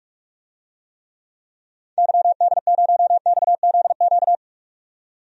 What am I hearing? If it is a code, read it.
Morse code, heard as a keyed tone, about 46 wpm, 700 Hz. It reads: YD0XZQ
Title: DXCC-100 Call Signs - 6 Characters - 45wpm